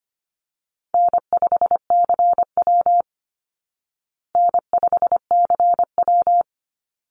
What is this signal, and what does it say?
Morse code, heard as a keyed tone, about 25 wpm, 700 Hz. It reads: N5CW N5CW
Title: Easy Call Signs Repeated 2x - 25wpm